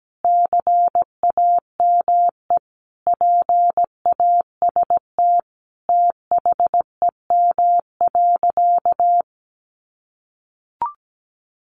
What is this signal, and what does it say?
Morse code, heard as a keyed tone, about 17 wpm, 700 Hz. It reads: CAME PAST THEM.
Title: Sentences from Top 600 Words 17wpm